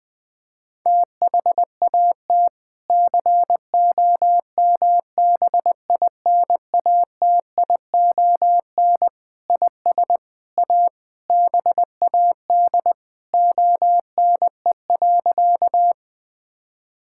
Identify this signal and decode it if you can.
Morse code, heard as a keyed tone, about 20 wpm, 700 Hz. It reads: THAT COMBINATION IS A BAD ONE.
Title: Sentences from Top 1000 Words - Review Part 6 - 20wpm